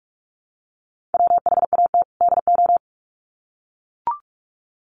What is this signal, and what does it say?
Morse code, heard as a keyed tone, about 44 wpm, 700 Hz. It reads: WHAT DO
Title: Sets of 2 Words 45wpm